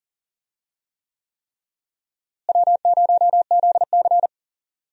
Morse code, heard as a keyed tone, about 40 wpm, 700 Hz. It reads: W0ZC
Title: USA Call Signs - 4 Characters - Repeated 3x - 40wpm